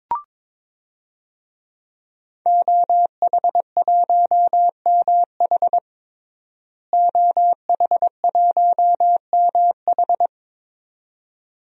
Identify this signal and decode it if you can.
Morse code, heard as a keyed tone, about 22 wpm, 700 Hz. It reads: OH1MH OH1MH
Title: DXCC-100 Call Signs - 5 Characters - Repeated 2x - 22wpm